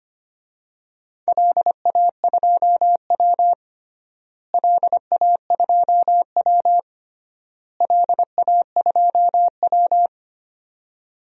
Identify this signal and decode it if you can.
Morse code, heard as a keyed tone, about 25 wpm, 700 Hz. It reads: LA2W LA2W LA2W
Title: DXCC-100 Call Signs - 4 Characters - Repeated 3x - 25wpm